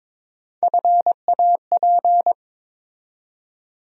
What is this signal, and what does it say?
Morse code, heard as a keyed tone, about 22 wpm, 700 Hz. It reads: FAP
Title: USA Call Sign Postfixes - 3 Letters 22wpm